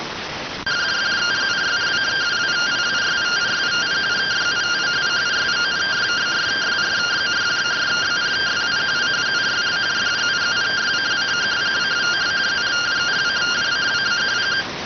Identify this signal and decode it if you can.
Two-tone teleprinter signal, about 170 Hz shift.